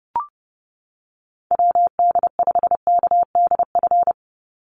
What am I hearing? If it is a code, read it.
Morse code, heard as a keyed tone, about 31 wpm, 700 Hz. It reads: WD5KDF
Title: USA Call Signs 30wpm